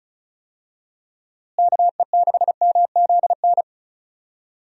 Morse code, heard as a keyed tone, about 36 wpm, 700 Hz. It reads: KE6MZN
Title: USA Call Signs - 6 Characters - Repeated 3x - 35wpm